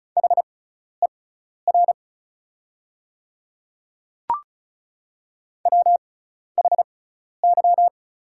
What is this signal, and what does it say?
Morse code, heard as a keyed tone, about 36 wpm, spaced as for 12 wpm, 690 Hz. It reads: HER WHY
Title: 6x Character Spacing - Top 500 Words - 35wpm